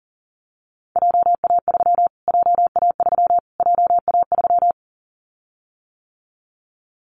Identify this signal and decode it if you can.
Morse code, heard as a keyed tone, about 39 wpm, 700 Hz. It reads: JA3 JA3 JA3
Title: Callsign Prefixes for Top 25 DXCC - Repeated 3x - 40wpm